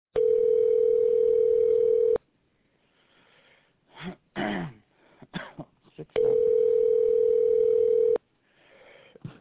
Ringback tone (North America), 440 + 480 Hz.